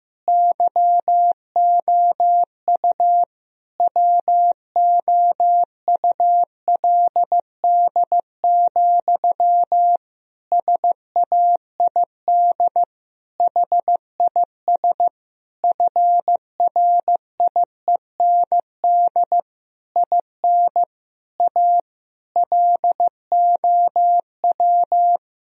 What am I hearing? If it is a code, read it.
Morse code, heard as a keyed tone, about 15 wpm, 700 Hz. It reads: YOU WOULD, SAID HIS FRIEND IN A LOW
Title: Sentences from Top 1000 Words - Review Part 1 - 15wpm